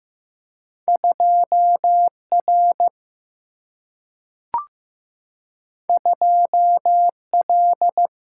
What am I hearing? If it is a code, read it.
Morse code, heard as a keyed tone, about 15 wpm, 700 Hz. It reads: 2R 2L
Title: Single-Letter Single-Number 15wpm